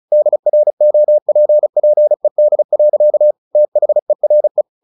Morse code, heard as a keyed tone, about 35 wpm, 600 Hz. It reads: DROPPED. THERE
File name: the old man and the sea 2 35wpm